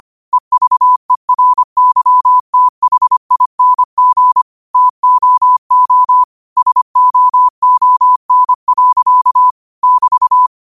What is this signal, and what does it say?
Morse code, heard as a keyed tone, about 25 wpm, 1000 Hz. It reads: EVERYTHING TOO SOON. <BT>
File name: qotdfiles / 25wpm